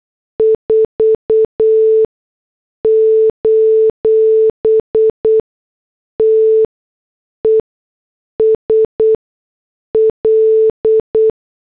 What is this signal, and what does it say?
Morse code, heard as a keyed tone, about 8 wpm, 440 Hz. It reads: 4 : T E S L
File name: morse / headlines